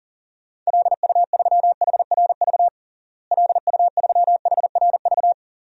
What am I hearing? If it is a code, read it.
Morse code, heard as a keyed tone, about 40 wpm, 700 Hz. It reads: LU3HRV LU3HRV
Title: DXCC-100 Call Signs - 6 Characters - Repeated 3x - 40wpm